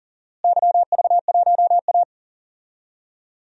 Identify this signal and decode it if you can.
Morse code, heard as a keyed tone, about 40 wpm, 700 Hz. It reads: YV1A
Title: DXCC-100 Call Signs - 4 Characters - Repeated 2x - 40wpm